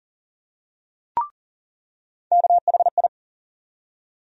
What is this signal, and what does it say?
Morse code, heard as a keyed tone, about 40 wpm, 700 Hz. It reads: KHI
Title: USA Call Sign Postfixes - 3 Letters 40wpm